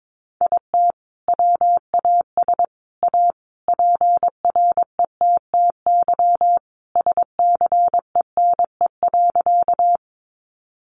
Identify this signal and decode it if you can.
Morse code, heard as a keyed tone, about 22 wpm, 700 Hz. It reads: IT WAS A PRETTY SCENE.
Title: Sentences from Top 1100 Words 22wpm